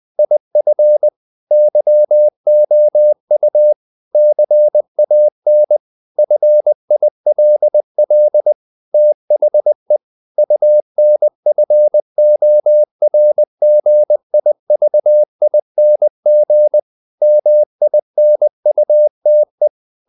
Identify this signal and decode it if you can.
Morse code, heard as a keyed tone, about 20 wpm, 600 Hz. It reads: IF YOU CAN FILL THE UNFORGIVING MINUTE
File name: CW / Practice / If20